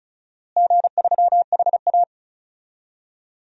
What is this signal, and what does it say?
Morse code, heard as a keyed tone, about 36 wpm, 700 Hz. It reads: G3HA
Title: DXCC-100 Call Signs - 4 Characters - Repeated 3x - 35wpm